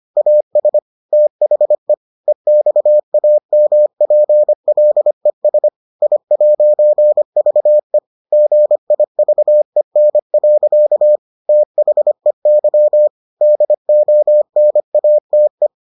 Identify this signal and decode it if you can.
Morse code, heard as a keyed tone, about 25 wpm, 600 Hz. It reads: AS THE EXAMPLES I'VE GIVEN. THEY DONATE